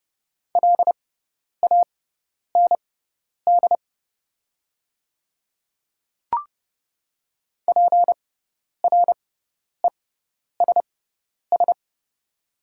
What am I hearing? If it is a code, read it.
Morse code, heard as a keyed tone, about 30 wpm, spaced as for 10 wpm, 700 Hz. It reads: LAND PRESS
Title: 6x Character Spacing - Top 300 Words - 30wpm